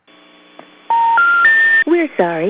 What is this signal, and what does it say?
Special information tone, with speech.